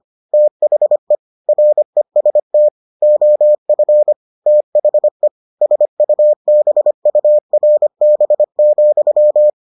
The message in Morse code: THE REST OF THE SUBURB,

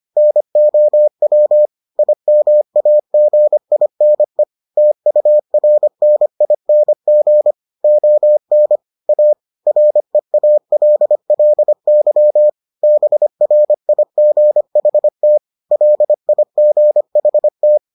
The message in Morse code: NOW IMAGINE TURNING ON A REALLY BRIGHT LIGHT

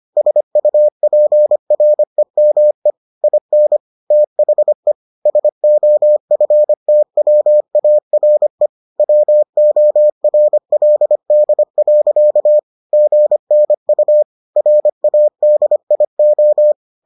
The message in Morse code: SUPREME IN THE SOFTWARE WORLD. GNU RADIO

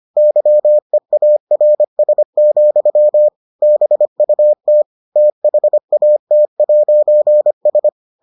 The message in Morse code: YEARS, BUT THAT'S

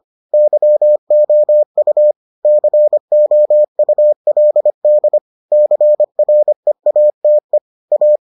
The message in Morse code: YOU COULD CREATE A

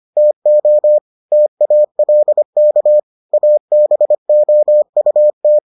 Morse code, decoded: TO TALK ABOUT

